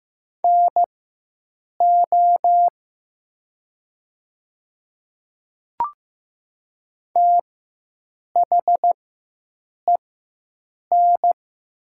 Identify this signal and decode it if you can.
Morse code, heard as a keyed tone, about 15 wpm, spaced as for 7 wpm, 700 Hz. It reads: NO THEN
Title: 4x Character Spacing - Top 100 Words - 15wpm